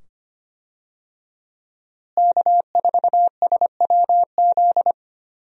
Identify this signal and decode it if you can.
Morse code, heard as a keyed tone, about 25 wpm, 700 Hz. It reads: K4SWZ
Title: USA Call Signs - 5 Characters - Repeated 2x - 25wpm